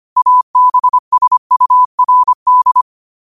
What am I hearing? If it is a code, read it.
Morse code, heard as a keyed tone, about 25 wpm, 1000 Hz. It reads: ABSURD
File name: qotdfiles / 25wpm